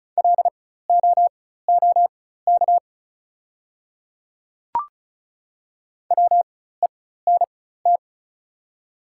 Morse code, heard as a keyed tone, about 34 wpm, spaced as for 16 wpm, 690 Hz. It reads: LOOK WENT
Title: 4x Character Spacing - Top 200 Words - 35wpm